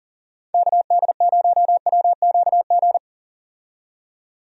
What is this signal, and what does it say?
Morse code, heard as a keyed tone, about 40 wpm, 700 Hz. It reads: KD0WQG
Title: USA Call Signs - 6 Characters - 40wpm